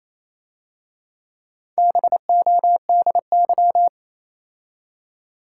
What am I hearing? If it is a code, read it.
Morse code, heard as a keyed tone, about 27 wpm, 700 Hz. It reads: BODY
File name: Easy Words - Speed-Racing with Hints - 22wpm